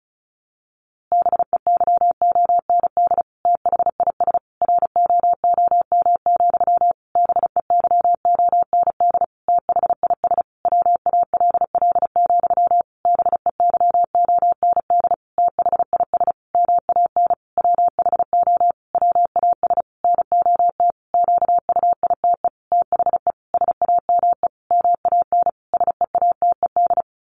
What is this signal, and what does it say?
Morse code, heard as a keyed tone, about 34 wpm, 700 Hz. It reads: BEYOND THIS ROOM, BEYOND THIS WALL, BEYOND THIS MAN WHO WAS NOT QUITE THE SAME MAN SEATED
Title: A Sound Of Thunder 35wpm